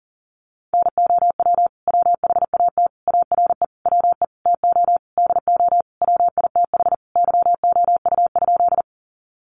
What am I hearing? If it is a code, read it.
Morse code, heard as a keyed tone, about 40 wpm, 700 Hz. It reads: NOW WHAT ARE WE TO DO WITH YOU?
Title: Sentences from Top 100 Words 40wpm